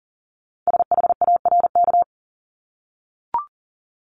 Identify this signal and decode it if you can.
Morse code, heard as a keyed tone, about 40 wpm, 700 Hz. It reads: SHARK
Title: Multi-letter Phonemes Repeated 3x 40wpm